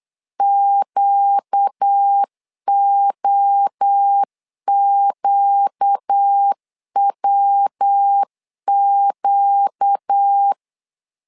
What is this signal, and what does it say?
Morse code, heard as a keyed tone, about 8 wpm, 800 Hz. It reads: QOQWQ
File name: LEZIONI / 4 Q 8wpm 2732